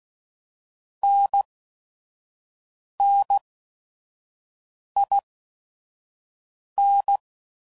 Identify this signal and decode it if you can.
Morse code, heard as a keyed tone, about 16 wpm, 800 Hz. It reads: N N I N